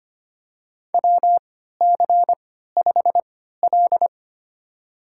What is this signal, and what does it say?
Morse code, heard as a keyed tone, about 25 wpm, spaced as for 14 wpm, 700 Hz. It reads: WC5L
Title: USA Call Signs - 4 Characters - 3x Character Spacing - 25wpm